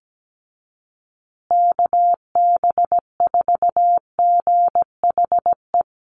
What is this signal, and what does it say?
Morse code, heard as a keyed tone, about 17 wpm, 700 Hz. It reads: KB4GHE
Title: USA Call Signs Repeated 3x 17wpm